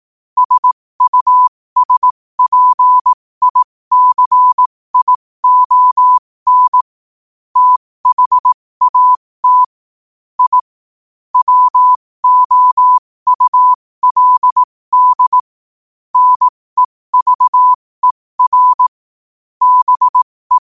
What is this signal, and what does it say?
Morse code, heard as a keyed tone, about 18 wpm, spaced as for 15 wpm, 1000 Hz. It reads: SUSPICION THAT I WOULD NEVER BE
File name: qotdfiles / 15wpm